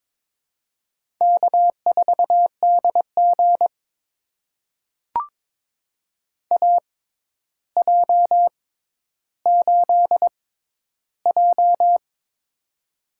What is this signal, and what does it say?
Morse code, heard as a keyed tone, about 22 wpm, spaced as for 11 wpm, 700 Hz. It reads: K4DG AJ8J
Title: USA Call Signs - 4 Characters - 6x Character Spacing - 22wpm